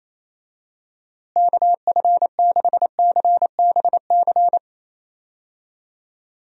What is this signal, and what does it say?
Morse code, heard as a keyed tone, about 28 wpm, 700 Hz. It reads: KF6CBC
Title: USA Call Signs - 6 Characters - Repeated 2x - 28wpm